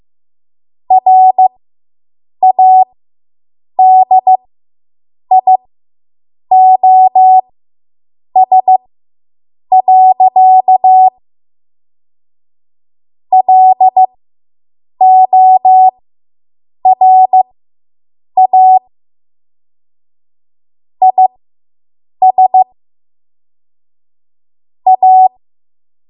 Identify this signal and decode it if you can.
Morse code, heard as a keyed tone, about 15 wpm, spaced as for 7 wpm, 750 Hz. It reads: RADIOS. LORA IS A